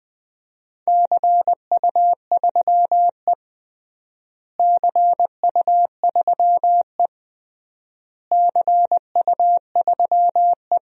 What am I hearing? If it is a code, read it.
Morse code, heard as a keyed tone, about 21 wpm, 700 Hz. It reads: CU3E CU3E CU3E
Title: DXCC-100 Call Signs - 4 Characters - Repeated 3x - 20wpm